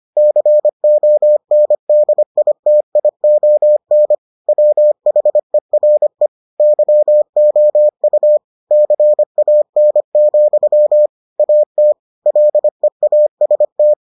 Morse code, heard as a keyed tone, about 25 wpm, 600 Hz. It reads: CONDITION WHERE YOU CAN, AT LEAST